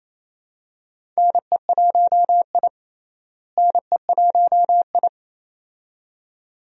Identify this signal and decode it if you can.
Morse code, heard as a keyed tone, about 27 wpm, 700 Hz. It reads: NE1I NE1I
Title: USA Call Signs - 4 Characters - Repeated 2x - 28wpm